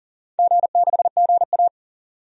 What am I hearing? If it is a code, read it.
Morse code, heard as a keyed tone, about 40 wpm, 700 Hz. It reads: G6GA
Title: DXCC-100 Call Signs - 4 Characters - 40wpm